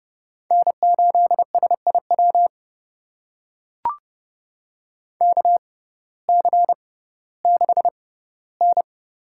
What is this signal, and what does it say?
Morse code, heard as a keyed tone, about 30 wpm, spaced as for 17 wpm, 700 Hz. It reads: N8SIW KC6N
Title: USA Call Signs - 6x Character Spacing - 30wpm